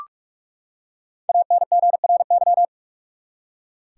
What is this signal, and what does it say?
Morse code, heard as a keyed tone, about 46 wpm, 700 Hz. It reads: ANGRY
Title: Five Letter Words - 45wpm